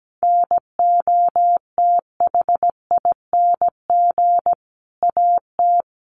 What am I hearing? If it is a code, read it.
Morse code, heard as a keyed tone, about 17 wpm, 700 Hz. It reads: NOTHING AT
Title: Sentences from Top 500 Words 17wpm